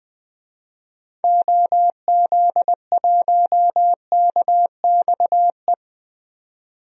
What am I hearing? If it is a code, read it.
Morse code, heard as a keyed tone, about 20 wpm, 700 Hz. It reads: OZ1KXE
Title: DXCC-100 Call Signs - 6 Characters - 20wpm